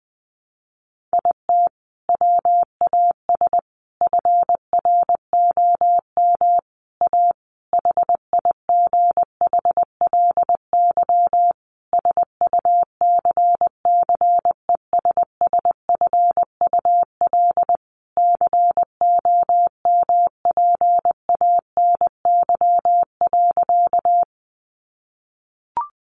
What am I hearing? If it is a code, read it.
Morse code, heard as a keyed tone, about 20 wpm, 700 Hz. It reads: IT WAS FROM A HIGHLY SUCCESSFUL COMPANY.
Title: Sentences from Top 1100 Words 20wpm